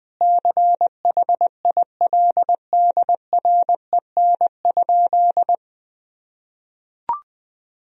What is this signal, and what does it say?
Morse code, heard as a keyed tone, about 20 wpm, 700 Hz. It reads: CHILDREN?
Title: Sentences from Top 1000 Words - Review Part 3 - 20wpm